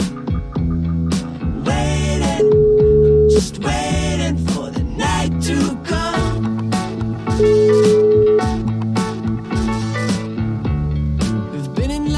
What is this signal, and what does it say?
Ringback tone (Europe, 425 Hz), with noise.